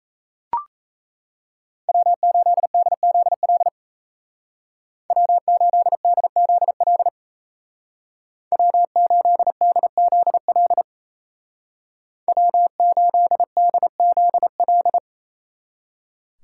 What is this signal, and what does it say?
Morse code, heard as a keyed tone, about 34 wpm, 700 Hz. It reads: W8DZL W8DZL W8DZL W8DZL
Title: Speed Racing - USA 1x3 Call Signs 28wpm